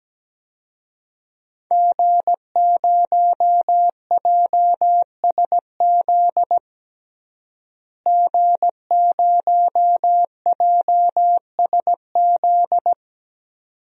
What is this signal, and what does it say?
Morse code, heard as a keyed tone, about 17 wpm, 700 Hz. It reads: G0JSZ G0JSZ
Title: DXCC-100 Call Signs - 5 Characters - Repeated 2x - 17wpm